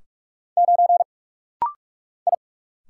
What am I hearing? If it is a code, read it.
Morse code, heard as a keyed tone, about 44 wpm, 700 Hz. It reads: 9 I
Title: Single Letter-Number - Rapid-Fire 45wpm